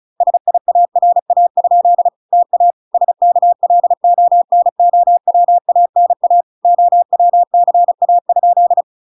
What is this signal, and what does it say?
Morse code, heard as a keyed tone, about 34 wpm, 700 Hz. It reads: SIARA? TA SKLONOWANA OWCA?